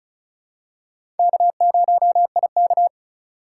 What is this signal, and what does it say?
Morse code, heard as a keyed tone, about 36 wpm, 700 Hz. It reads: K0IK
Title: USA Call Signs - 4 Characters - 3x Character Spacing - 35wpm